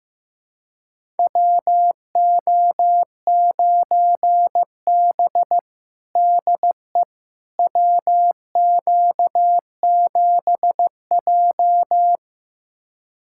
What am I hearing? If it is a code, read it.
Morse code, heard as a keyed tone, about 15 wpm, 700 Hz. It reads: WO9B DE WQ7J
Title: USA Callsign DE Callsign - 4 Characters - 15wpm